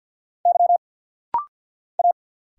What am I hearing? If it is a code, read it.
Morse code, heard as a keyed tone, about 46 wpm, 690 Hz. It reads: Y A
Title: Single Letters - Rapid-Fire 50wpm